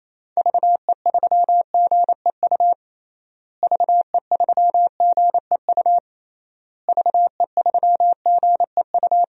Morse code, heard as a keyed tone, about 27 wpm, 700 Hz. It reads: VE3GEU VE3GEU VE3GEU
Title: DXCC-100 Call Signs - 6 Characters - Repeated 3x - 28wpm